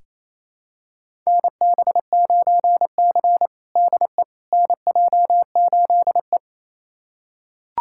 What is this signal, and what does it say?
Morse code, heard as a keyed tone, about 28 wpm, 700 Hz. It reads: NB9C DE NJ8E
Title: USA Callsign DE Callsign - 4 Characters - 28wpm